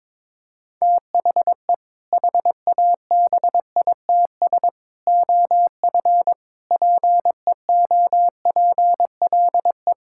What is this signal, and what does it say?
Morse code, heard as a keyed tone, about 22 wpm, 700 Hz. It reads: THE HABITS OF PEOPLE
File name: practice / 22wpm / Sentences with 2-Word Phrases - 22wpm